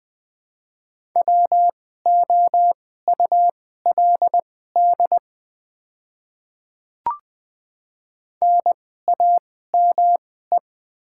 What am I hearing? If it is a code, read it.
Morse code, heard as a keyed tone, about 20 wpm, spaced as for 14 wpm, 700 Hz. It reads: WOULD NAME
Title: 2x Character Spacing - Top 200 Words - 20wpm